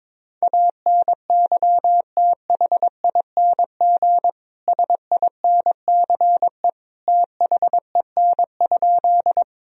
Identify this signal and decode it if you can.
Morse code, heard as a keyed tone, about 22 wpm, 700 Hz. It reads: ANYTHING SINCE THEN?
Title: Sentences from Top 1000 Words - Review Part 1 - 22wpm